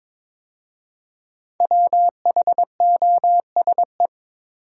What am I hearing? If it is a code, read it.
Morse code, heard as a keyed tone, about 22 wpm, 700 Hz. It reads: WHOSE
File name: 5 Letter Words - First 3 Letters Given - 22wpm